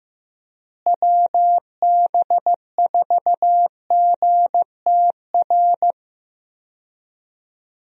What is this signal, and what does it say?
Morse code, heard as a keyed tone, about 16 wpm, 700 Hz. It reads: WB4GTR
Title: USA Call Signs - 6 Characters - Repeated 3x - 15wpm